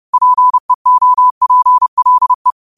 Morse code, heard as a keyed tone, about 30 wpm, 1000 Hz. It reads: PEOPLE
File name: qotdfiles / 30wpm